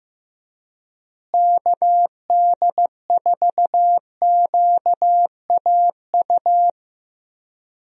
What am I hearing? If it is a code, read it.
Morse code, heard as a keyed tone, about 15 wpm, 700 Hz. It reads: KD4QAU
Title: USA Call Signs - 6 Characters - Repeated 3x - 15wpm